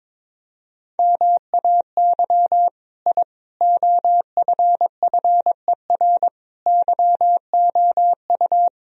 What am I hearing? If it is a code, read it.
Morse code, heard as a keyed tone, about 22 wpm, 700 Hz. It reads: MAY I OFFER YOU
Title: Sentences from Top 1000 Words - Review Part 4 - 22wpm